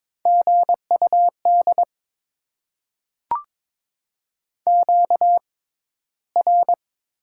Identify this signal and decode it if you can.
Morse code, heard as a keyed tone, about 22 wpm, spaced as for 13 wpm, 700 Hz. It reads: GUD QR
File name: Top 20 QSO Elements - 6x Character Spacing - 22wpm